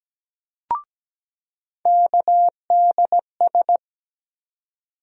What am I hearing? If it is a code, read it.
Morse code, heard as a keyed tone, about 17 wpm, 700 Hz. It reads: KDS
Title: USA Call Sign Postfixes - 3 Letters 17wpm